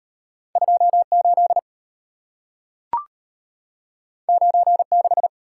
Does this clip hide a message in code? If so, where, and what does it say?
Yes: Morse code from 0.6 to 5.3 s, heard as a keyed tone, about 33 wpm, 700 Hz. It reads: 2 8 9 6